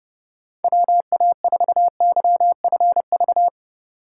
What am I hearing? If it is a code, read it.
Morse code, heard as a keyed tone, about 30 wpm, 700 Hz. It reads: WA4YFV
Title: USA Call Signs - 6 Characters - 4x Character Spacing - 30wpm